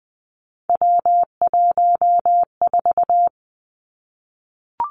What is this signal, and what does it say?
Morse code, heard as a keyed tone, about 20 wpm, 700 Hz. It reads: W14
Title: Three Challenging Letter-Number Combinations 20wpm